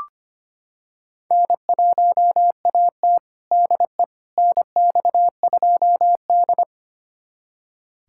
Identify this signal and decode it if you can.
Morse code, heard as a keyed tone, about 25 wpm, 700 Hz. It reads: N1AT DE NX2D
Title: USA Callsign DE Callsign - 4 Characters - 25wpm